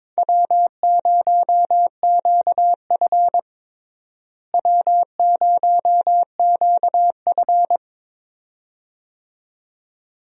Morse code, heard as a keyed tone, about 22 wpm, 700 Hz. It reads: W0QF W0QF
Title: USA Call Signs - 4 Characters - Repeated 2x - 22wpm